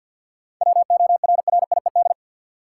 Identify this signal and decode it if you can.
Morse code, heard as a keyed tone, about 53 wpm, 700 Hz. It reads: WORRIED
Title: Seven Letter Words - 50wpm